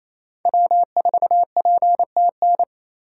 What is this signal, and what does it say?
Morse code, heard as a keyed tone, about 28 wpm, 700 Hz. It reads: W4PTN
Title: USA Call Signs - 5 Characters - 4x Character Spacing - 28wpm